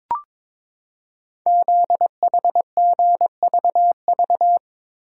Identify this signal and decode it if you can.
Morse code, heard as a keyed tone, about 23 wpm, 700 Hz. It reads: ZHGVV
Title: Code Groups - 5 Letters - 22wpm